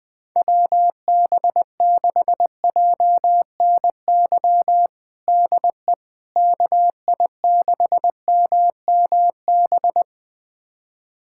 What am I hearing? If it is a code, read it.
Morse code, heard as a keyed tone, about 20 wpm, 700 Hz. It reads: WB6JNY DE KI6MMB
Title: USA Callsign DE Callsign 20wpm